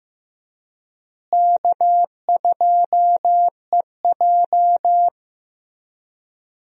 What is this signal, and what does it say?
Morse code, heard as a keyed tone, about 15 wpm, 700 Hz. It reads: K2EJ
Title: USA Call Signs - 4 Characters - Repeated 3x - 15wpm